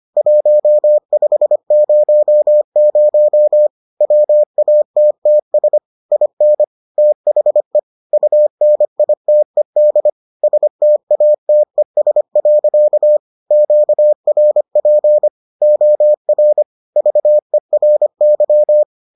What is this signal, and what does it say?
Morse code, heard as a keyed tone, about 25 wpm, 600 Hz. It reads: 1500 WATTS IN THE UNITED STATES. QRP OR VERY